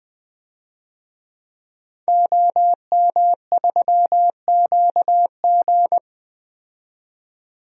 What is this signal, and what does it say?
Morse code, heard as a keyed tone, about 21 wpm, 700 Hz. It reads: OM3QG
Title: DXCC-100 Call Signs - 5 Characters - 20wpm